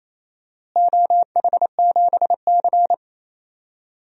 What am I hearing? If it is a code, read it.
Morse code, heard as a keyed tone, about 28 wpm, 700 Hz. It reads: OH7C
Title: DXCC-100 Call Signs - 4 Characters - Repeated 2x - 28wpm